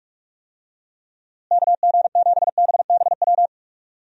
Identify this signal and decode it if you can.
Morse code, heard as a keyed tone, about 46 wpm, 700 Hz. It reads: KG7BBW
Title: USA Call Signs - 6 Characters - Repeated 3x - 45wpm